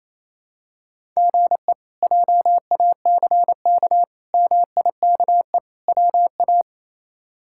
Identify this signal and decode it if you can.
Morse code, heard as a keyed tone, about 28 wpm, 700 Hz. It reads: GE JACK MIKE WA
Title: K1USN SST QSO Practice - 28wpm